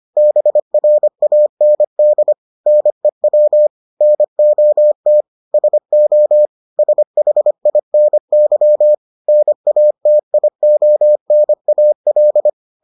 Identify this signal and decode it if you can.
Morse code, heard as a keyed tone, about 25 wpm, 600 Hz. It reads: BRAND NEW NOT SO SHINY NATIONAL